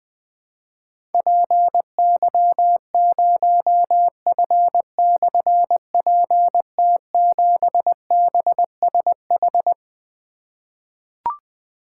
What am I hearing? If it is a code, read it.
Morse code, heard as a keyed tone, about 20 wpm, 700 Hz. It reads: PY0F/PT7BSH
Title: Difficult Call Signs 20wpm